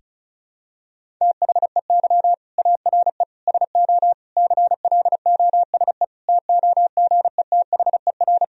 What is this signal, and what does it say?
Morse code, heard as a keyed tone, about 36 wpm, 700 Hz. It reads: THEY ARE SO CLOSE TOGETHER
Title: Sentences from Top 1000 Words - Review Part 1 - 35wpm